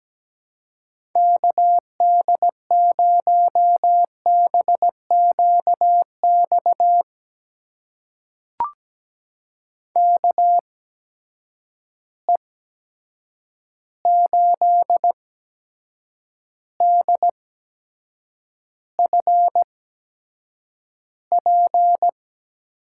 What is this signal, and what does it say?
Morse code, heard as a keyed tone, about 17 wpm, spaced as for 7 wpm, 700 Hz. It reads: KD0BQX KE8DFP
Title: USA Call Signs - 6 Characters - 8x Character Spacing - 17wpm